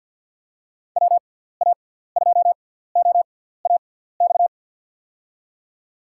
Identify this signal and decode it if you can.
Morse code, heard as a keyed tone, about 52 wpm, spaced as for 17 wpm, 700 Hz. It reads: WA2OAX
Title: USA Call Signs - 6 Characters - 6x Character Spacing - 50wpm